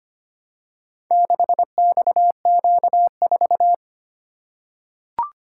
Morse code, heard as a keyed tone, about 25 wpm, 700 Hz. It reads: 6XQ4
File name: practice / 25wpm / Four Letter-Number Combinations - 25wpm